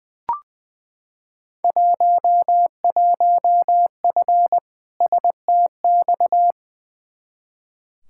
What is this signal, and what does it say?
Morse code, heard as a keyed tone, about 20 wpm, 700 Hz. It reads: 11F STX
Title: Field Day Exchanges Encore - 20wpm